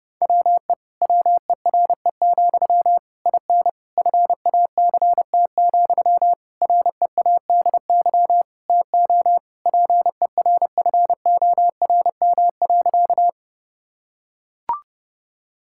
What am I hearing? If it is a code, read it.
Morse code, heard as a keyed tone, about 30 wpm, 700 Hz. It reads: WE WERE, IN FACT, READY TO PERFORM.
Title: Sentences from Top 1000 Words - Review Part 5 - 30wpm